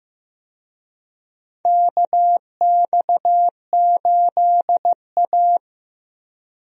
Morse code, heard as a keyed tone, about 15 wpm, 700 Hz. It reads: KX8A
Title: USA Call Signs - 4 Characters - 15wpm